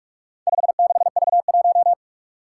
Morse code, heard as a keyed tone, about 44 wpm, 700 Hz. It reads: 56V1
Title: Four Challenging Letter-Number Combinations 45wpm